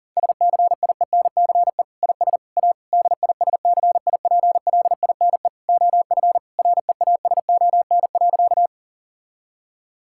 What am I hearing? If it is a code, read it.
Morse code, heard as a keyed tone, about 40 wpm, 700 Hz. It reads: SCIENCE IS A DISCIPLINE OF REASON.
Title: Sentences from Top 1000 Words - Review Part 6 - 40wpm